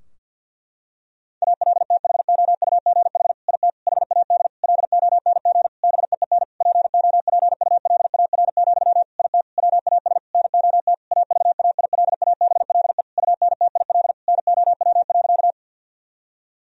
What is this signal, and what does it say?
Morse code, heard as a keyed tone, about 46 wpm, 700 Hz. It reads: ALTHOUGH IT HAD LONG BEEN POPULAR, IT WAS NOT AVAILABLE UNTIL NOW.
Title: Sentences from Top 1000 Words - Review Part 3 - 50wpm